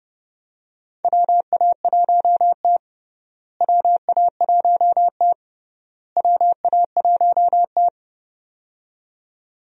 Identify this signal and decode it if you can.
Morse code, heard as a keyed tone, about 32 wpm, 700 Hz. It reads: WA1T WA1T WA1T
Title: USA Call Signs - 4 Characters - Repeated 3x - 30wpm